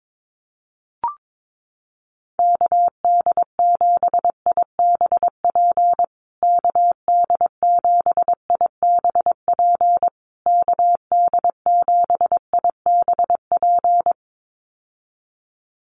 Morse code, heard as a keyed tone, about 22 wpm, 700 Hz. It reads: KD7IBP KD7IBP KD7IBP
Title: USA Call Signs Repeated 3x 22wpm